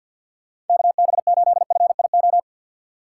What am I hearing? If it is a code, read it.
Morse code, heard as a keyed tone, about 50 wpm, 690 Hz. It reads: KB8FIO
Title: USA Call Signs - 6 Characters - 8x Character Spacing - 50wpm